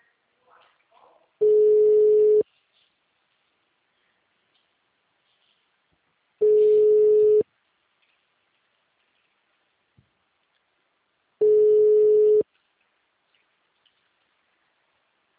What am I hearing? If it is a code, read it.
Ringback tone (Europe, 425 Hz), with speech.
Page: Itu - SP